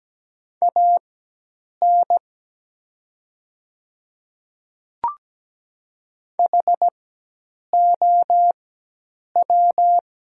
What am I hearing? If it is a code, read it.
Morse code, heard as a keyed tone, about 17 wpm, spaced as for 8 wpm, 700 Hz. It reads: AN HOW
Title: 4x Character Spacing - Top 100 Words - 17wpm